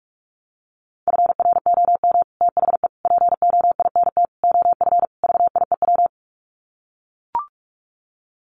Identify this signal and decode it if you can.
Morse code, heard as a keyed tone, about 44 wpm, 700 Hz. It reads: FROM THE POINT OF VIEW
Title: Sets of 5 Words 45wpm